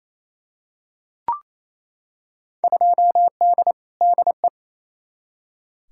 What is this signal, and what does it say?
Morse code, heard as a keyed tone, about 28 wpm, 700 Hz. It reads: 2D DE
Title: Field Day Exchanges Encore - 28wpm